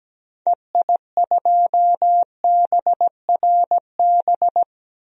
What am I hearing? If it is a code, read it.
Morse code, heard as a keyed tone, about 17 wpm, 700 Hz. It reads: EI2BRB
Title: DXCC-100 Call Signs - 6 Characters - 17wpm